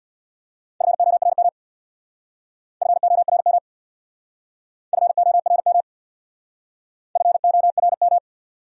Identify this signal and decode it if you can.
Morse code, heard as a keyed tone, about 64 wpm, 700 Hz. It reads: FORM FORM FORM FORM